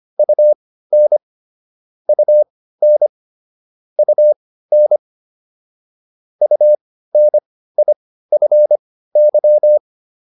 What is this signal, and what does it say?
Morse code, heard as a keyed tone, about 25 wpm, spaced as for 15 wpm, 600 Hz. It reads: UN UN UN UNIFY